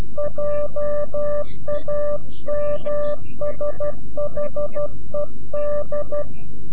Morse code, heard as a keyed tone, about 12 wpm, 590 Hz. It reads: JAMSHED
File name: CW-QSO / CQ Call / CQ-QSO